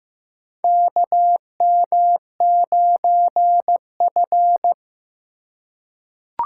Morse code, heard as a keyed tone, about 15 wpm, 700 Hz. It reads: KM9F
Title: USA Call Signs - 4 Characters - 3x Character Spacing - 15wpm